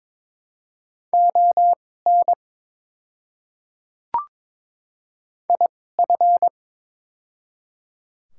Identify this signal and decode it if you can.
Morse code, heard as a keyed tone, about 22 wpm, spaced as for 16 wpm, 700 Hz. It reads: ON IF
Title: 2x Character Spacing - Top 200 Words - 22wpm